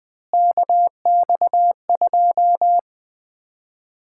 Morse code, heard as a keyed tone, about 21 wpm, 700 Hz. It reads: KX2
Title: Common QSO Phrases - 4x Word-Spacing - 20wpm